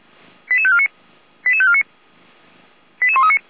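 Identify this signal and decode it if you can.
Five-tone selective call (ZVEI), heard as quick runs of five tones, tones of about 70 ms. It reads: 88549 88549 88139
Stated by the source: ZVEI five-tone call, code 88341, received by SDR; Feuerwehr Orlamünde.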